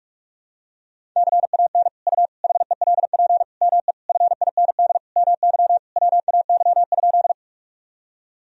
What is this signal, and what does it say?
Morse code, heard as a keyed tone, about 46 wpm, 700 Hz. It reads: CAN U HELP ME FIND MY WAY?
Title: Sentences from Top 200 Words and Abbreviations 45wpm